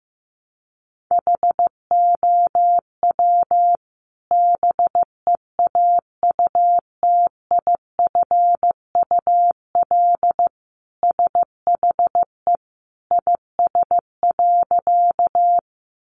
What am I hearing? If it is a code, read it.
Morse code, heard as a keyed tone, about 15 wpm, 700 Hz. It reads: HOW BEAUTIFUL SHE IS.
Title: Sentences from Top 900 Words 15wpm